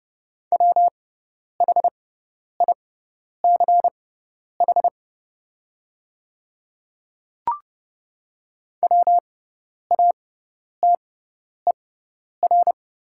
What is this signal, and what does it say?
Morse code, heard as a keyed tone, about 30 wpm, spaced as for 10 wpm, 700 Hz. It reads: WHICH WATER
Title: 6x Character Spacing - Top 100 Words - 30wpm